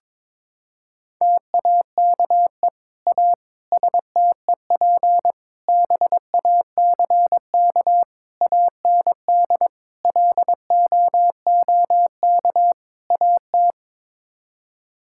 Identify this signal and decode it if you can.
Morse code, heard as a keyed tone, about 22 wpm, 700 Hz. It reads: TAKE A STEP BACK AND LOOK AT
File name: practice / 22wpm / Sets of 7 Words - 22wpm